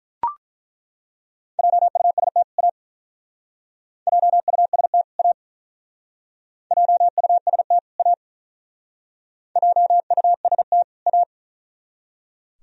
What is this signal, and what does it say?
Morse code, heard as a keyed tone, about 43 wpm, 700 Hz. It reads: JUST A JUST A JUST A JUST A
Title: Speed-Racing - Sets of 2 Words 35wpm